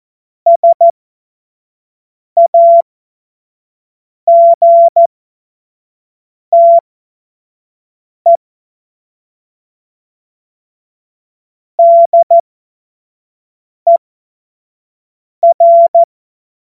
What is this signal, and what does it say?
Morse code, heard as a keyed tone, about 14 wpm, spaced as for 5 wpm, 680 Hz. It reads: SAGTE DER